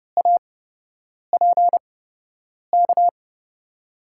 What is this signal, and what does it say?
Morse code, heard as a keyed tone, about 30 wpm, spaced as for 8 wpm, 700 Hz. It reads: APK
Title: USA Call Signs - 5 Characters - 8x Character Spacing - 30wpm